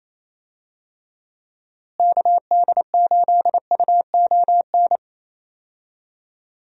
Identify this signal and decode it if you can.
Morse code, heard as a keyed tone, about 28 wpm, 700 Hz. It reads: KD8UON
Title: USA Call Signs - 6 Characters - Repeated 2x - 28wpm